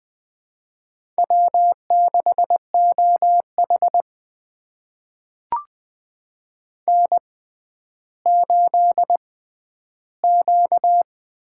Morse code, heard as a keyed tone, about 20 wpm, spaced as for 11 wpm, 700 Hz. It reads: W6OH N8Q
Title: USA Call Signs - 4 Characters - 6x Character Spacing - 20wpm